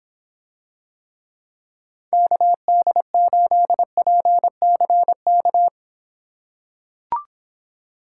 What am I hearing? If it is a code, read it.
Morse code, heard as a keyed tone, about 26 wpm, 700 Hz. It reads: KD8PCK